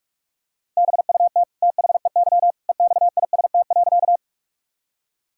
Morse code, heard as a keyed tone, about 47 wpm, 700 Hz. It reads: BUT THEY EXIST.
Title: Sentences from Top 1000 Words - Review Part 5 - 45wpm